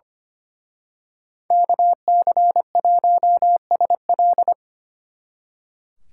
Morse code, heard as a keyed tone, about 29 wpm, 700 Hz. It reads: KC1SL
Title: Speed Racing - USA Call Signs 25wpm